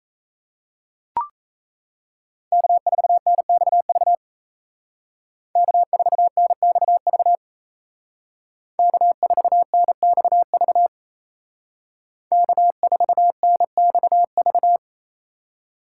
Morse code, heard as a keyed tone, about 36 wpm, 700 Hz. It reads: K4NXV K4NXV K4NXV K4NXV